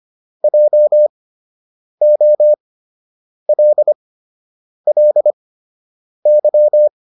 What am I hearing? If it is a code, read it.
Morse code, heard as a keyed tone, about 25 wpm, spaced as for 8 wpm, 600 Hz. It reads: JOLLY